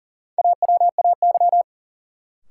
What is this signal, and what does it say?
Morse code, heard as a keyed tone, about 40 wpm, 700 Hz. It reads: AWAY
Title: Sets of 2 Words - New Word Drill - 40wpm